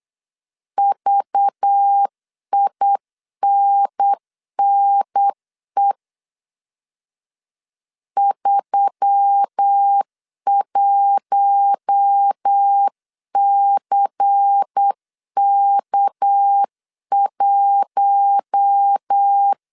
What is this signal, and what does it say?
Morse code, heard as a keyed tone, about 8 wpm, 800 Hz. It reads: VINNE 31CK1
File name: CW / DET / 1 6 1500C S C 8wpm 5219